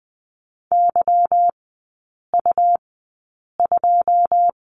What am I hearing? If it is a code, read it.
Morse code, heard as a keyed tone, about 20 wpm, spaced as for 8 wpm, 700 Hz. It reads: YU2
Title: 152 Lesson 25e - Letter p - Three Characters 20wpm